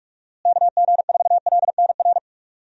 Morse code, heard as a keyed tone, about 46 wpm, 700 Hz. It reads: KG4LNR
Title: USA Call Signs - 6 Characters - Repeated 3x - 45wpm